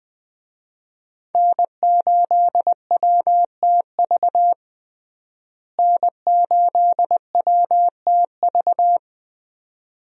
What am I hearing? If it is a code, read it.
Morse code, heard as a keyed tone, about 20 wpm, 700 Hz. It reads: N8WTV N8WTV
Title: USA Call Signs - 5 Characters - Repeated 2x - 20wpm